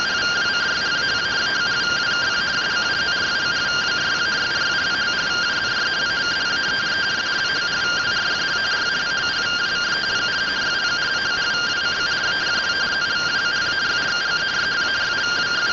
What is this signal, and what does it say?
Two-tone teleprinter signal, about 170 Hz shift.
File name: findings / 100 surview 0.gen-26 rtty rx